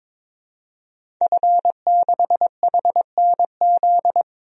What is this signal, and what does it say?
Morse code, heard as a keyed tone, about 22 wpm, 700 Hz. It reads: F6HNZ
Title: DXCC-100 Call Signs - 5 Characters - 22wpm